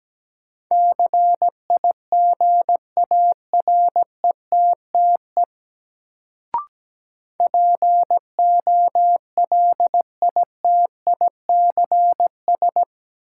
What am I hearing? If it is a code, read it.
Morse code, heard as a keyed tone, about 17 wpm, 700 Hz. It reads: CIGARETTE POLITICS
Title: Sentences from Top 1000 Words - New Word Drill - 17wpm